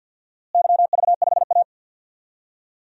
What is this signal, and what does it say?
Morse code, heard as a keyed tone, about 49 wpm, 700 Hz. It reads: YV5A
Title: DXCC-100 Call Signs - 4 Characters - Repeated 2x - 50wpm